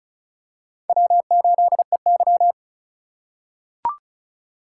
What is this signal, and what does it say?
Morse code, heard as a keyed tone, about 36 wpm, 700 Hz. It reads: W8EY
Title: USA Call Signs - 4 Characters - 2x Character Spacing - 35wpm